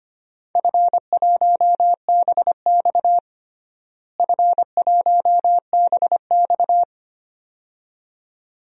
Morse code, heard as a keyed tone, about 25 wpm, 700 Hz. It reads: F1BX F1BX
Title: DXCC-100 Call Signs - 4 Characters - Repeated 2x - 25wpm